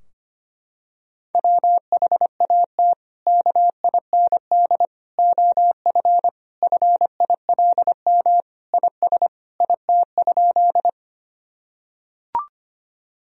Morse code, heard as a keyed tone, about 25 wpm, 700 Hz. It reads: WHAT KIND OF FILM IS IT?
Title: Sentences from Top 1000 Words - Review Part 3 - 25wpm